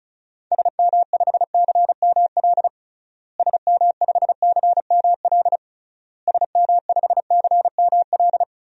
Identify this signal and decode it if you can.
Morse code, heard as a keyed tone, about 36 wpm, 700 Hz. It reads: SM5CML SM5CML SM5CML
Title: DXCC-100 Call Signs - 6 Characters - Repeated 3x - 35wpm